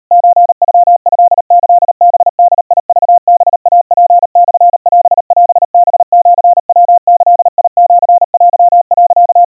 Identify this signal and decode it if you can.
Morse code (Japanese Wabun code), heard as a keyed tone, about 39 wpm, 700 Hz. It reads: ソノトキハボクハイツモオオハシヤギシテ、
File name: kawa / Morse sample / kamyu 140 cpm 700 Hz Crispy